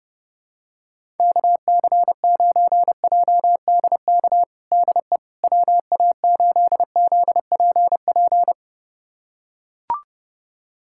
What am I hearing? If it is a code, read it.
Morse code, heard as a keyed tone, about 30 wpm, 700 Hz. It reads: KC9JDK DE WA8ZPP
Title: USA Callsign DE Callsign - 6 Characters - 30wpm